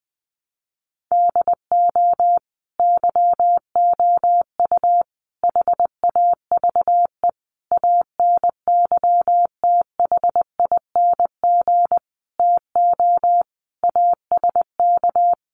Morse code, heard as a keyed tone, about 20 wpm, 700 Hz. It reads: DO YOU HAVE ANYTHING TO ASK
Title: Sentences from Top 1000 Words 20wpm